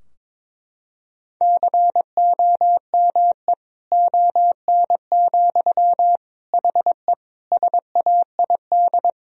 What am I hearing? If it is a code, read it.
Morse code, heard as a keyed tone, about 22 wpm, 700 Hz. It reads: COME ON, HE SAID
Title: Sentences from Top 1000 Words - Review Part 2 - 22wpm